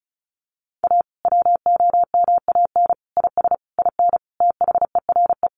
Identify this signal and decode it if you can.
Morse code, heard as a keyed tone, about 34 wpm, 700 Hz. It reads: A WOMAN IS IN THERE
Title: Sentences from Top 1400 Words 35wpm